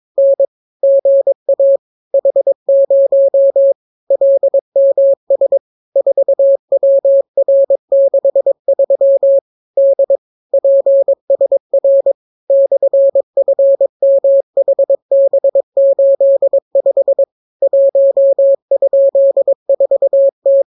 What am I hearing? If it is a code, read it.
Morse code, heard as a keyed tone, about 22 wpm, 550 Hz. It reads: N GA H0 LMS 4WR63 D PSR /FMHB85 1?4T